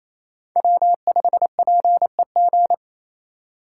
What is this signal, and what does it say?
Morse code, heard as a keyed tone, about 28 wpm, 700 Hz. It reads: W5PEG
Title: USA Call Signs - 5 Characters - 4x Character Spacing - 28wpm